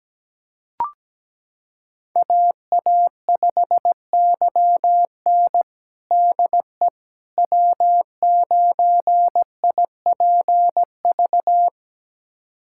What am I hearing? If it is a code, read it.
Morse code, heard as a keyed tone, about 17 wpm, 700 Hz. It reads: AA5YN DE W9IPV
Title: USA Callsign DE Callsign - 5 Characters - 17wpm